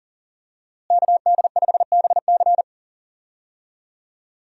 Morse code, heard as a keyed tone, about 40 wpm, 700 Hz. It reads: KD5BC
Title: USA Call Signs - 5 Characters - Repeated 2x - 40wpm